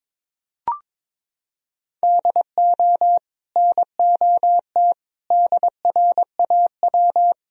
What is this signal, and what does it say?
Morse code, heard as a keyed tone, about 22 wpm, 700 Hz. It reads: DO NOT DRAW
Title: Sentences from Top 1000 Words - Review Part 1 - 22wpm